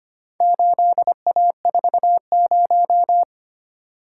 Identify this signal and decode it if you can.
Morse code, heard as a keyed tone, about 25 wpm, 700 Hz. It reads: 8A40
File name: practice / 25wpm / Four Letter-Number Combinations - 25wpm